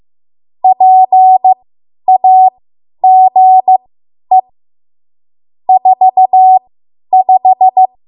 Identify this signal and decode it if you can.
Morse code, heard as a keyed tone, about 15 wpm, spaced as for 10 wpm, 750 Hz. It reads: PAGE 45
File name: Morse / 10WPM